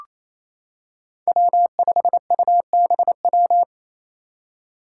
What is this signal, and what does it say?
Morse code, heard as a keyed tone, about 29 wpm, 700 Hz. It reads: W5UBW
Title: Five Challenging Letter-Number Combinations 28wpm